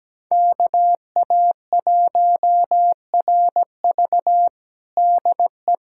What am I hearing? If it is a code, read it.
Morse code, heard as a keyed tone, about 17 wpm, 700 Hz. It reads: KA1RV DE
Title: USA Callsign DE Callsign - 5 Characters - 17wpm